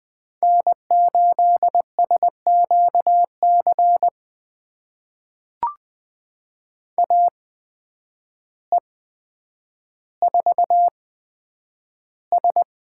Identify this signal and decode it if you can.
Morse code, heard as a keyed tone, about 20 wpm, spaced as for 9 wpm, 700 Hz. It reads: N8SQC AE4S
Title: USA Call Signs - 5 Characters - 8x Character Spacing - 20wpm